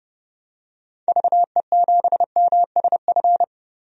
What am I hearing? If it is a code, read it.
Morse code, heard as a keyed tone, about 30 wpm, 700 Hz. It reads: VE7MSF
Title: DXCC-100 Call Signs - 6 Characters - Repeated 2x - 30wpm